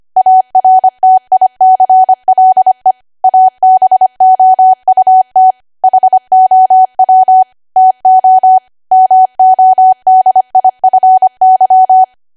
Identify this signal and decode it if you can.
Morse code, heard as a keyed tone, about 25 wpm, 750 Hz. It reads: ARTICLE ABOUT HOW TO MODIFY
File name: Archive / 25 WPM / 25WPM